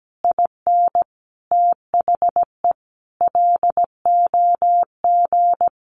Morse code, heard as a keyed tone, about 17 wpm, 700 Hz. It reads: IN THE LOG
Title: Sentences from Top 1400 Words 17wpm